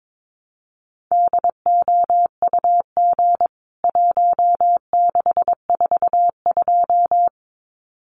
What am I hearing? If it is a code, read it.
Morse code, heard as a keyed tone, about 22 wpm, 700 Hz. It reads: DOUG 1642
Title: CWT Exchange 22wpm